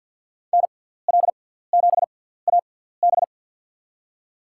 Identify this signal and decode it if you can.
Morse code, heard as a keyed tone, about 47 wpm, spaced as for 17 wpm, 700 Hz. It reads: NL7AB
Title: USA Call Signs - 6x Character Spacing - 50wpm